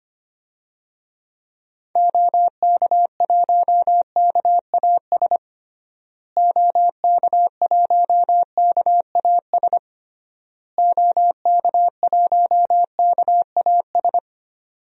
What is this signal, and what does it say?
Morse code, heard as a keyed tone, about 25 wpm, 700 Hz. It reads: OK1KAS OK1KAS OK1KAS
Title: DXCC-100 Call Signs - 6 Characters - Repeated 3x - 25wpm